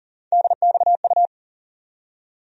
Morse code, heard as a keyed tone, about 40 wpm, 700 Hz. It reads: DXU
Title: USA Call Sign Postfixes - 3 Letters 40wpm